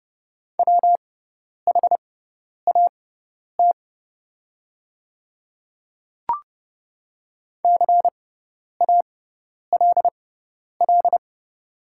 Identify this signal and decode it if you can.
Morse code, heard as a keyed tone, about 30 wpm, spaced as for 10 wpm, 700 Hz. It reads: WHAT CALL
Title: 6x Character Spacing - Top 100 Words - 30wpm